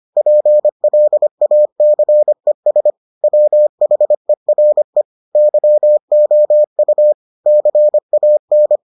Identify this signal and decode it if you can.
Morse code, heard as a keyed tone, about 25 wpm, 600 Hz. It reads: PLACES WHERE YOU CAN